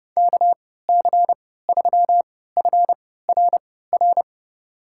Morse code, heard as a keyed tone, about 32 wpm, spaced as for 17 wpm, 700 Hz. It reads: KC3FRR
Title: USA Call Signs - 6 Characters - 3x Character Spacing - 30wpm